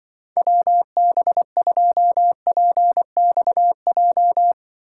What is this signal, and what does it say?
Morse code, heard as a keyed tone, about 24 wpm, 700 Hz. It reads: WB2PXJ
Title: Speed Racing - USA 2x3 Call Signs 20wpm